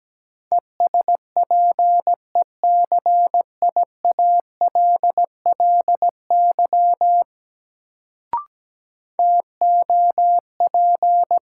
Morse code, heard as a keyed tone, about 17 wpm, 700 Hz. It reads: ESPECIALLY TOP
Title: Sentences from Top 700 Words - New Word Drill - 17wpm